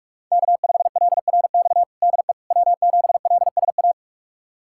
Morse code, heard as a keyed tone, about 43 wpm, 700 Hz. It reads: K5LRX DE W7LSA
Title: USA Callsign DE Callsign 45wpm